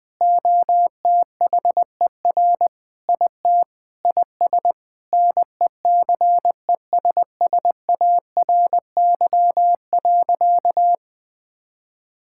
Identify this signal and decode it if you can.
Morse code, heard as a keyed tone, about 20 wpm, 700 Hz. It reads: OTHER IT IS NECESSARY.
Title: Sentences with 8-Word Phrases - 20wpm